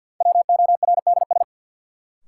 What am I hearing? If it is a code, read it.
Morse code, heard as a keyed tone, about 53 wpm, 700 Hz. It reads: WORDS
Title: Sets of 3 Words - New Word Drill - 50wpm